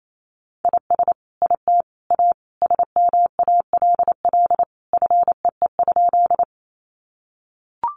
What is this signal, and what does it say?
Morse code, heard as a keyed tone, about 28 wpm, 700 Hz. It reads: IS IT A SMALL FEE?
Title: Sentences from Top 1500 Words 28wpm